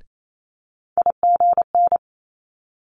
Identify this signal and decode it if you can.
Morse code, heard as a keyed tone, about 28 wpm, 700 Hz. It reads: IGN
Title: Multi-letter Phonemes Quick Fire 28wpm